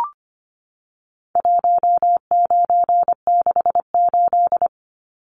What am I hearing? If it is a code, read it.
Morse code, heard as a keyed tone, about 25 wpm, 700 Hz. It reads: 1968
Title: Numbers - 4-Digit 25wpm